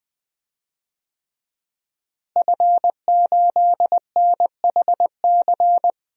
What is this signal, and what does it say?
Morse code, heard as a keyed tone, about 20 wpm, 700 Hz. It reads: F8NHC
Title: DXCC-100 Call Signs - 5 Characters - Repeated 3x - 20wpm